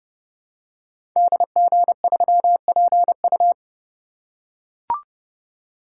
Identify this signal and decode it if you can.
Morse code, heard as a keyed tone, about 29 wpm, 700 Hz. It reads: DG3PU
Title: Code Groups - 5 Letter:Numbers - 30wpm